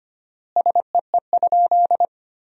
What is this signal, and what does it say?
Morse code, heard as a keyed tone, about 25 wpm, 700 Hz. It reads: SEE?
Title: 4x Word-Spacing - Sentences from Top 100 Words - 25wpm